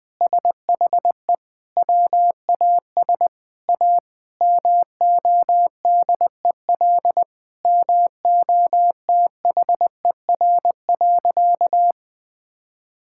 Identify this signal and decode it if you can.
Morse code, heard as a keyed tone, about 19 wpm, 700 Hz. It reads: SHE WAS A MODEL MOTHER.
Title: Sentences from Top 1000 Words - Review Part 5 - 20wpm